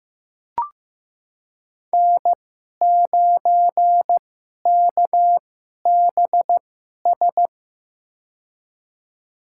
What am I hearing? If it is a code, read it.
Morse code, heard as a keyed tone, about 15 wpm, spaced as for 11 wpm, 700 Hz. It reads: N9KBS
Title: USA Call Signs - 5 Characters - 2x Character Spacing - 15wpm